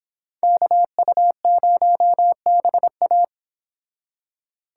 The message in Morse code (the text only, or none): KU0BA